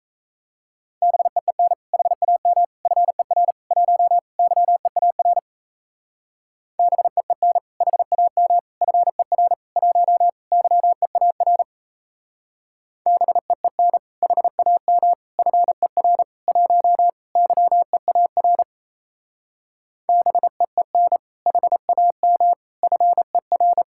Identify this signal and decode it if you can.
Morse code, heard as a keyed tone, about 35 wpm, 700 Hz. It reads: BEEN HAM FER 1 YEAR BEEN HAM FER 1 YEAR BEEN HAM FER 1 YEAR BEEN HAM FER
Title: Speed-Racing Common QSO Phrases Encore - 28wpm